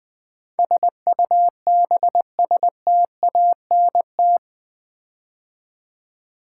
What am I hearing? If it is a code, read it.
Morse code, heard as a keyed tone, about 20 wpm, 700 Hz. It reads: SUBSTANT
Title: Build A Word - 4 Letters At A Time - 11-Letter Words - 20wpm